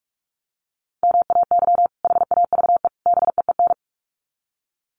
Morse code, heard as a keyed tone, about 44 wpm, 700 Hz. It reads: MAY HAVE BEEN
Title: Sets of 3 Words 45wpm